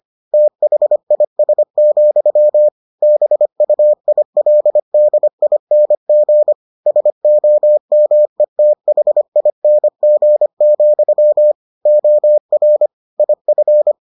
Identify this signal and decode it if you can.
Morse code, heard as a keyed tone, about 25 wpm, 600 Hz. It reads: THIS, BUILDING SOMETHING, OR IF